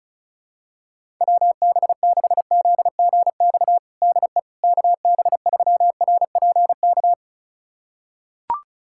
Morse code, heard as a keyed tone, about 36 wpm, 700 Hz. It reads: WB6ZGX DE KB3RPK
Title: USA Callsign DE Callsign 35wpm